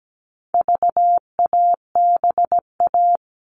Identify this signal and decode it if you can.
Morse code, heard as a keyed tone, about 17 wpm, 700 Hz. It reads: VABA
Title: All Cut Numbers - 4-Digit 17wpm